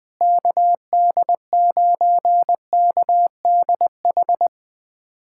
Morse code, heard as a keyed tone, about 20 wpm, 700 Hz. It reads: KD9KDH
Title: USA Callsign DE Callsign - 6 Characters - 20wpm